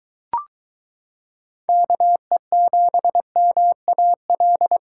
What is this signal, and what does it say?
Morse code, heard as a keyed tone, about 22 wpm, 700 Hz. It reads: KE7MAL